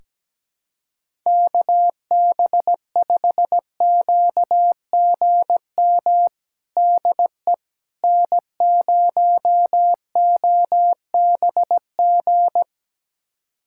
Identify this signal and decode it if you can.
Morse code, heard as a keyed tone, about 17 wpm, 700 Hz. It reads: KB5QGM DE N0OBG
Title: USA Callsign DE Callsign 17wpm